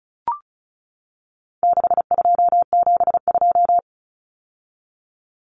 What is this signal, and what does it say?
Morse code, heard as a keyed tone, about 34 wpm, 700 Hz. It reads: 6272
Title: Numbers - 4-Digit 35wpm